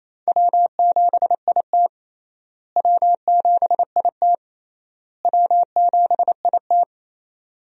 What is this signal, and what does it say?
Morse code, heard as a keyed tone, about 28 wpm, 700 Hz. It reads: W7IT W7IT W7IT
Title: USA Call Signs - 4 Characters - Repeated 3x - 28wpm